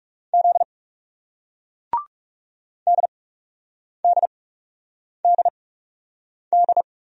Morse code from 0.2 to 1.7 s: Z

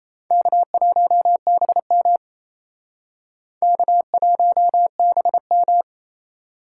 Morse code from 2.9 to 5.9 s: K1BM

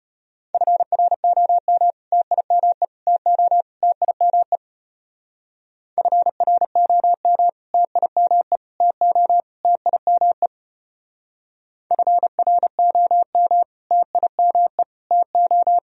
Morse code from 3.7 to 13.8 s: TIME FROM TIME TO TIME FROM

